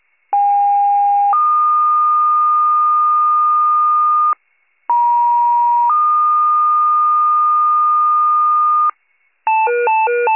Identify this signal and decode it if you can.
Two-tone page, heard as a one-second tone, then a three-second tone, about 810 then 1220 Hz, about 950 then 1220 Hz. It is heard from 0.3 to 8.9 s.